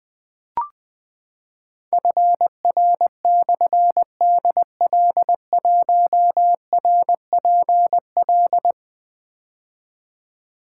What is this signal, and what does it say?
Morse code, heard as a keyed tone, about 20 wpm, 700 Hz. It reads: FR/DL1RPL
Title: Difficult Call Signs 20wpm